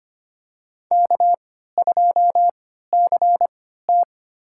Morse code, heard as a keyed tone, about 25 wpm, spaced as for 14 wpm, 700 Hz. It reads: K2CT
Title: USA Call Signs - 4 Characters - 3x Character Spacing - 25wpm